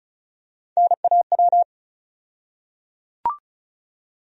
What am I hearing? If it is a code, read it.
Morse code, heard as a keyed tone, about 36 wpm, 700 Hz. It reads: NAW
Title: USA Call Sign Postfixes - 3 Letters 35wpm